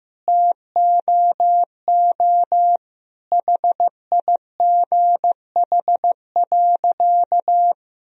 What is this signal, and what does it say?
Morse code, heard as a keyed tone, about 15 wpm, 700 Hz. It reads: TOO HIGH.
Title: Sentences from Top 1000 Words - Review Part 1 - 15wpm